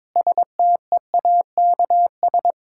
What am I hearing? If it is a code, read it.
Morse code, heard as a keyed tone, about 22 wpm, 700 Hz. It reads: STEAKS
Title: Pangram Challenge v2 - 22wpm